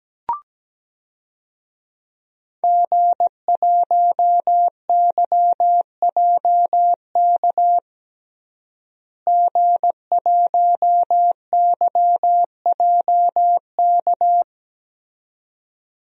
Morse code, heard as a keyed tone, about 17 wpm, 700 Hz. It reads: G1YJK G1YJK
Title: DXCC-100 Call Signs - 5 Characters - Repeated 2x - 17wpm